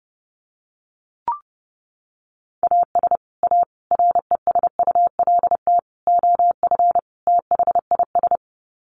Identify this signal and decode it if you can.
Morse code, heard as a keyed tone, about 30 wpm, 700 Hz. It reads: AS A RESULT OF THIS
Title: Sets of 5 Words 30wpm